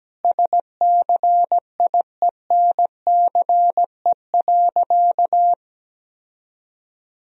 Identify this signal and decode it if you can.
Morse code, heard as a keyed tone, about 17 wpm, 700 Hz. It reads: SCIENCE.
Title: Sentences from Top 1000 Words - Review Part 2 - 17wpm